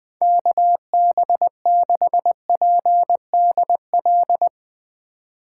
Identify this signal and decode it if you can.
Morse code, heard as a keyed tone, about 20 wpm, 700 Hz. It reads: KB6PDL
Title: Speed Racing - USA Call Signs 17wpm